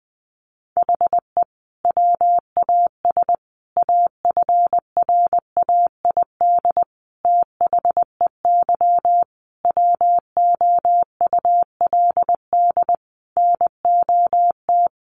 Morse code, heard as a keyed tone, about 20 wpm, 700 Hz. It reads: HE WAS AFRAID THEY WOULD NOT
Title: Sentences from Top 1500 Words 20wpm